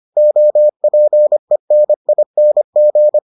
Morse code, heard as a keyed tone, about 25 wpm, 600 Hz. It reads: OPENING